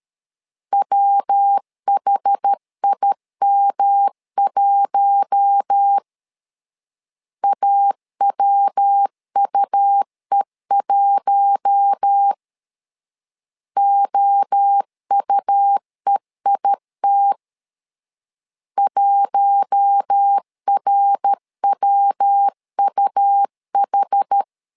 Morse code, heard as a keyed tone, about 13 wpm, 800 Hz. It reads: WHIM1 AWUE1 OUEIT 1RWUH
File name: LEZIONI / 4 Q1 12wpm 2028